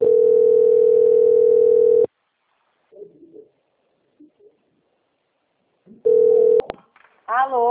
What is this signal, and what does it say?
Ringback tone (North America), 440 + 480 Hz.
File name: out- - - - - .26181